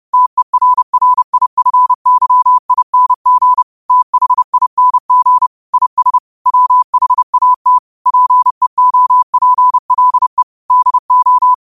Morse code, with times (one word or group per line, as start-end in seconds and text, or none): 0.1-3.6 TERRIFYING
3.9-5.5 THING
5.7-6.2 IS
6.5-7.8 WHAT
8.1-10.4 PEOPLE
10.7-11.5 DO